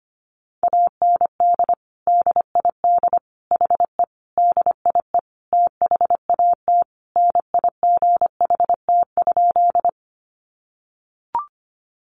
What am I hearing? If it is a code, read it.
Morse code, heard as a keyed tone, about 25 wpm, 700 Hz. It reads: AND DID HE DIE THAT NIGHT?
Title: Sentences from Top 1300 Words 25wpm